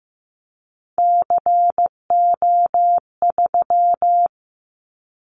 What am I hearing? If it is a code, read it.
Morse code, heard as a keyed tone, about 15 wpm, 700 Hz. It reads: CO3
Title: Callsign Prefixes for Top 100 DXCC - 15wpm